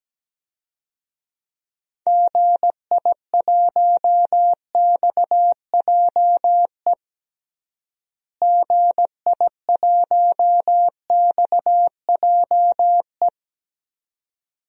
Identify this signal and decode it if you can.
Morse code, heard as a keyed tone, about 17 wpm, 700 Hz. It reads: GI1XJE GI1XJE
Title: DXCC-100 Call Signs - 6 Characters - Repeated 2x - 17wpm